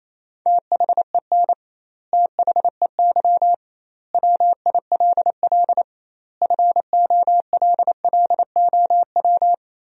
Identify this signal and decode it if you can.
Morse code, heard as a keyed tone, about 28 wpm, 700 Hz. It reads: THEN THEY WILL FOLLOW
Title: 2x Word-Spacing - Sentences from Top 200 Words - 28wpm